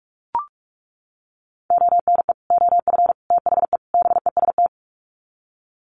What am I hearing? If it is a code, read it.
Morse code, heard as a keyed tone, about 44 wpm, 700 Hz. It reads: ONE OF THE BEST
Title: Sets of 4 Words 45wpm